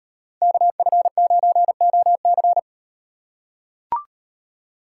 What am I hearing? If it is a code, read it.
Morse code, heard as a keyed tone, about 37 wpm, 700 Hz. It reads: KF9OC